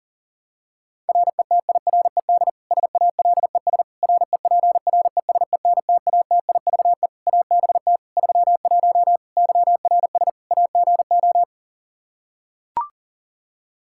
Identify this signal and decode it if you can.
Morse code, heard as a keyed tone, about 40 wpm, 700 Hz. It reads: RETIRED SALES REPRESENTATIVE ABT 31 YRS AGO
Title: Practice QSO - Exchange 3 - More Info - 40wpm